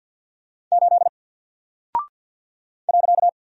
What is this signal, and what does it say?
Morse code, heard as a keyed tone, about 48 wpm, 700 Hz. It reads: 8 .
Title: Single Character (Letter:Number:Punctuation:Prosign) 50wpm